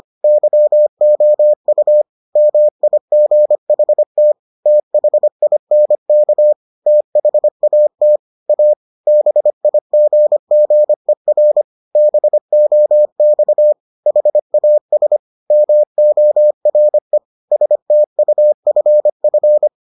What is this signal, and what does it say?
Morse code, heard as a keyed tone, about 25 wpm, 600 Hz. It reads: YOU MIGHT THINK THAT A BIGGER BOX HAS MORE STUFF